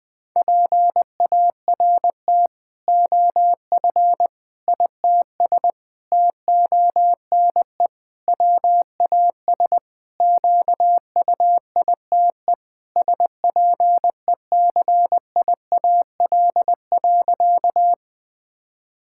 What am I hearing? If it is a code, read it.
Morse code, heard as a keyed tone, about 20 wpm, 700 Hz. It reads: PART OF ITS TONE WAS QUITE SPECIAL.
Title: Sentences from Top 1600 Words 20wpm